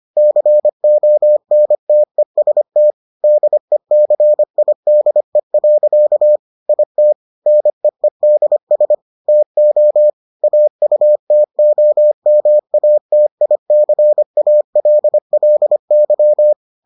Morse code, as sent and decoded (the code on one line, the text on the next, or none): -.-. --- -. - . ... - / -.. -.-. .. -.. .-.-.- / .. - / -. . -.. ... / - --- / .- ..- - --- -- .- - .. -.-. .- .-.. .-.. -.--
CONTEST DECIDE. IT NEEDS TO AUTOMATICALLY